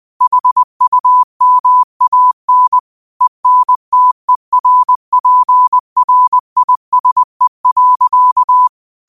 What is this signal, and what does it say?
Morse code, heard as a keyed tone, about 20 wpm, 1000 Hz. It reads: HUMAN ENTERPRISE.
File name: qotdfiles / 20wpm